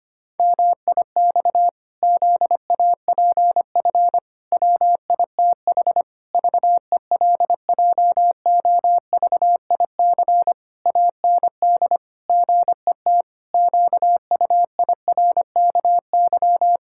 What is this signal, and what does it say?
Morse code, heard as a keyed tone, about 25 wpm, 700 Hz. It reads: MIX ZAPF WITH VELJOVIC AND GET QUIRKY
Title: Pangram Challenge v2 - 25wpm